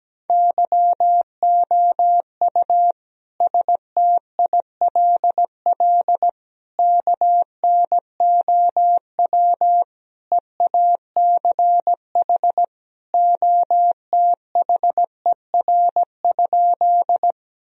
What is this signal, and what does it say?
Morse code, heard as a keyed tone, about 17 wpm, 700 Hz. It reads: YOU STILL KNOW EACH OTHER?
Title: Sentences from Top 1000 Words - Review Part 1 - 17wpm